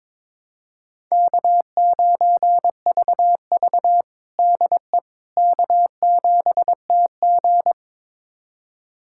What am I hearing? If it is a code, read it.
Morse code, heard as a keyed tone, about 22 wpm, 700 Hz. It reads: K9VV DE K7TG
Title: USA Callsign DE Callsign - 4 Characters - 22wpm